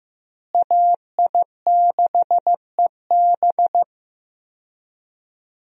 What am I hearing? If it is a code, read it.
Morse code, heard as a keyed tone, about 15 wpm, 700 Hz. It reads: AI6EB
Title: USA Call Signs - 5 Characters - Repeated 3x - 15wpm